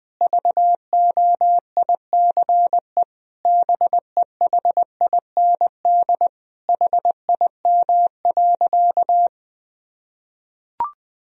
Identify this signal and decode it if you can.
Morse code, heard as a keyed tone, about 20 wpm, 700 Hz. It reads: VOICE BEHIND HIM.
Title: Sentences from Top 1000 Words - Review Part 1 - 20wpm